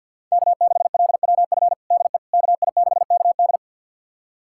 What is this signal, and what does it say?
Morse code, heard as a keyed tone, about 52 wpm, 700 Hz. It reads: K6LWF DE KI6KD
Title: USA Callsign DE Callsign - 5 Characters - 50wpm